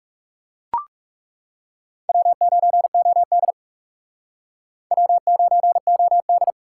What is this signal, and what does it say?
Morse code, heard as a keyed tone, about 38 wpm, 700 Hz. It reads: W9OD W9OD